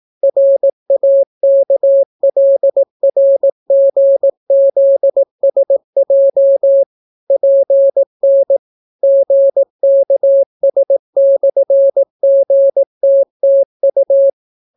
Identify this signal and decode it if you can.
Morse code, heard as a keyed tone, about 18 wpm, 550 Hz. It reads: RAKLRGZSJ PN GKS/GTTU